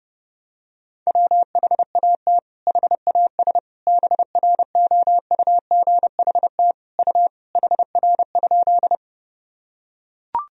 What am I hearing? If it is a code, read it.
Morse code, heard as a keyed tone, about 32 wpm, 700 Hz. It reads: WHAT HAS BROUGHT U HR?
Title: Sentences from Top 500 Words and Abbreviations 30wpm